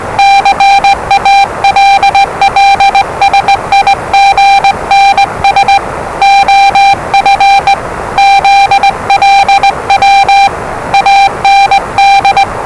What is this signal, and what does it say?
Morse code, heard as a keyed tone, about 20 wpm, 790 Hz, with noise. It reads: CALLSIGNS OF ZLW AND
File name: WellingtonRadioZlwCloseDownOnHfAndMf / 01-Zlw 4277khz 417.5 qrt until 0925z